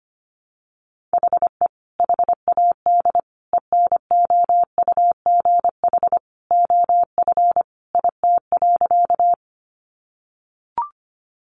Morse code, heard as a keyed tone, about 25 wpm, 700 Hz. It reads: HE HAD ENOUGH OF IT.
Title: Sentences from Top 400 Words 25wpm